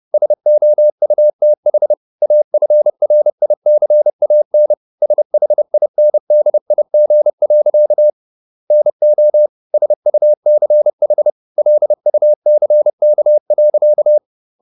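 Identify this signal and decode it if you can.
Morse code, heard as a keyed tone, about 30 wpm, 600 Hz. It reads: SOUTH AFRICAN SHINDIG. NO SUCH LUCK.
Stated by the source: Morse code at 30 wpm, the novel Never by Ken Follett in English, from CW med Gnister.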